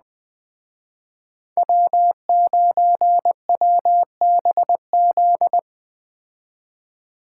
Morse code, heard as a keyed tone, about 20 wpm, 700 Hz. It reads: W9WBZ
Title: Speed Racing - USA 1x3 Call Signs 17wpm